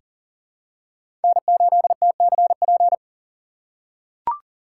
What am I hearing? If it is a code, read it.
Morse code, heard as a keyed tone, about 40 wpm, 700 Hz. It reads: N8TCP
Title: USA Call Signs - 5 Characters - 2x Character Spacing - 40wpm